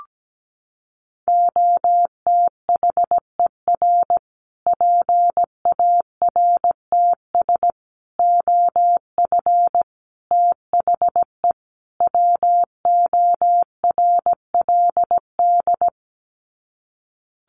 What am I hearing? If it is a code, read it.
Morse code, heard as a keyed tone, about 17 wpm, 700 Hz. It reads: OTHER PARTS OF THE WORLD
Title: Sets of 5 Words Encore 17wpm